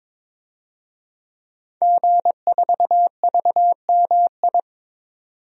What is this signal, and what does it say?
Morse code, heard as a keyed tone, about 22 wpm, 700 Hz. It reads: G4VMI
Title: DXCC-100 Call Signs - 5 Characters - 22wpm